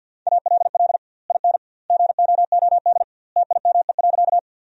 Morse code, heard as a keyed tone, about 53 wpm, 700 Hz. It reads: ALL IN GOOD TIME.
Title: 2x Word-Spacing - Sentences from Top 200 Words - 50wpm